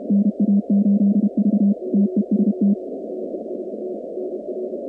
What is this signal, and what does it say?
Morse code, heard as a keyed tone, about 32 wpm, 200 Hz. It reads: NA8V TEST
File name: contests / 2019-cw-wpx-kt5j-k5pi- -2330 0-46